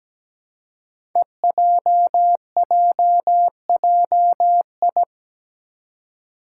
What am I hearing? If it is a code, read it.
Morse code, heard as a keyed tone, about 17 wpm, 700 Hz. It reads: EJJJI
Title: Five Challenging Letter-Number Combinations 17wpm